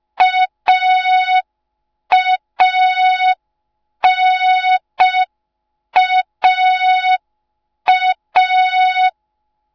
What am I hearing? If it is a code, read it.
Morse code, heard as a keyed tone, about 5 wpm, 3700 Hz. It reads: AANAA